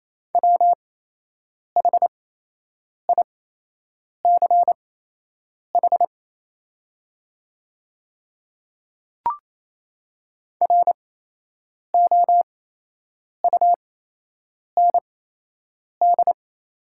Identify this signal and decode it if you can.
Morse code, heard as a keyed tone, about 28 wpm, spaced as for 8 wpm, 700 Hz. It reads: WHICH ROUND
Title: 8x Character Spacing - Top 300 Words - 28wpm